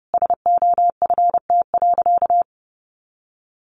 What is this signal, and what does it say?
Morse code, heard as a keyed tone, about 29 wpm, 700 Hz. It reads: SOFT.
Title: Easy Phrases 30wpm